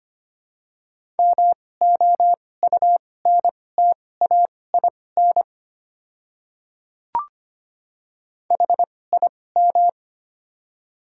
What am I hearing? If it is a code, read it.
Morse code, heard as a keyed tone, about 25 wpm, spaced as for 18 wpm, 700 Hz. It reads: MOUNTAIN HIM
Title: 2x Character Spacing - Top 300 Words - 25wpm